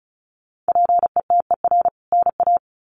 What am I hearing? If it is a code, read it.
Morse code, heard as a keyed tone, about 34 wpm, 700 Hz. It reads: PETER NA
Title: CWT Exchange 35wpm